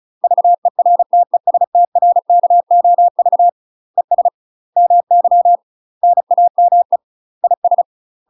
Morse code, heard as a keyed tone, about 34 wpm, 700 Hz. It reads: VERTESTRKOV ES MY NAME IS
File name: UK MASTER QSOs TRAINING / 35WPM / UK MASTER QSOs 0012